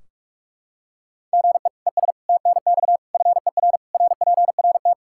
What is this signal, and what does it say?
Morse code, heard as a keyed tone, about 46 wpm, 700 Hz. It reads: GE ES TNX FER RPRT
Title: Practice QSO - Exchange 1 - Beginning - 45wpm